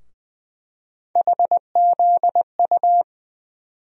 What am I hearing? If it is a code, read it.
Morse code, heard as a keyed tone, about 21 wpm, 700 Hz. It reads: HZU
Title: USA Call Sign Postfixes - 3 Letters 20wpm